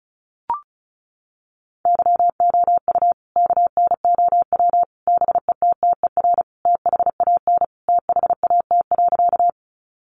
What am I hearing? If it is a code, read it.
Morse code, heard as a keyed tone, about 34 wpm, 700 Hz. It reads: YOU KNOW BETTER THAN THAT.
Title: Sentences from Top 400 Words 35wpm